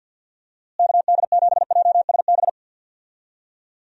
Morse code, heard as a keyed tone, about 46 wpm, 700 Hz. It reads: KD7JSB
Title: USA Call Signs - 6 Characters - Repeated 3x - 50wpm